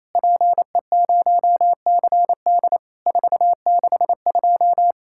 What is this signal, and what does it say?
Morse code, heard as a keyed tone, about 27 wpm, 700 Hz. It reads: PE0CD 462
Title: Worked All Europe DX Contest - QTC Exchange - 28wpm